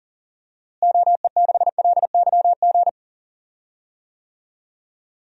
Morse code, heard as a keyed tone, about 40 wpm, 700 Hz. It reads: OE6LYG
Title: DXCC-100 Call Signs - 6 Characters - Repeated 2x - 40wpm